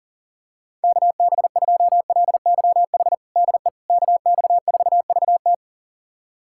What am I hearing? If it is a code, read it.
Morse code, heard as a keyed tone, about 40 wpm, 700 Hz. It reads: KB2LYH DE KX4VT
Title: USA Callsign DE Callsign 40wpm